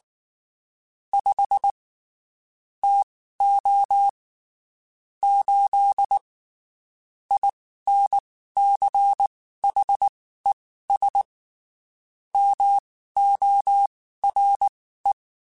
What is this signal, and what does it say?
Morse code, heard as a keyed tone, about 19 wpm, spaced as for 14 wpm, 800 Hz. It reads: 5 TO 8 INCHES MORE
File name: (spaced) / QSOs Spaced-mp3s / WA7QOX 1 @18 sp